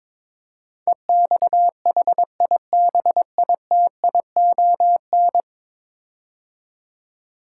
Morse code, heard as a keyed tone, about 22 wpm, 700 Hz. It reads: EXHIBITION
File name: practice / 22wpm / 10 Letter Words - First 3 Letters Given - 22wpm